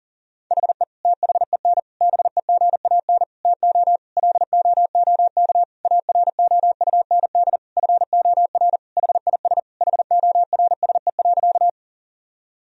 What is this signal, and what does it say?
Morse code, heard as a keyed tone, about 40 wpm, 700 Hz. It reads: HE THEN BEGAN TO LOOK AROUND FOR HIS HORSE.
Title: Sentences from Top 1000 Words - Review Part 3 - 40wpm